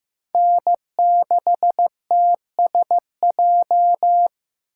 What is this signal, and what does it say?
Morse code, heard as a keyed tone, about 18 wpm, 700 Hz. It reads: N6TSJ
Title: Speed Racing - USA 1x3 Call Signs 15wpm